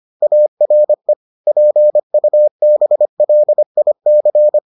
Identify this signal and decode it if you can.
Morse code, heard as a keyed tone, about 25 wpm, 600 Hz. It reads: ARE PUBLIC